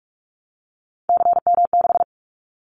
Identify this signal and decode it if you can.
Morse code, heard as a keyed tone, about 46 wpm, 700 Hz. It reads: CM6
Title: Callsign Prefixes for Top 100 DXCC - 45wpm